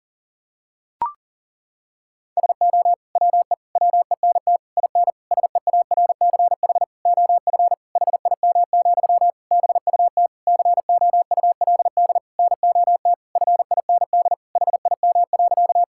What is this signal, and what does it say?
Morse code, heard as a keyed tone, about 40 wpm, 700 Hz. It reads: SO WE WENT IN SEARCH OF HIM, BUT COULD NOT FIND HIM.
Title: Sentences from Top 1000 Words - Review Part 5 - 40wpm